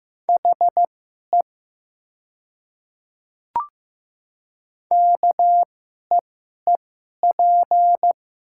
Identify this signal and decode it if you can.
Morse code, heard as a keyed tone, about 15 wpm, spaced as for 11 wpm, 700 Hz. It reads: HE KEEP
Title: 2x Character Spacing - Top 500 Words - 15wpm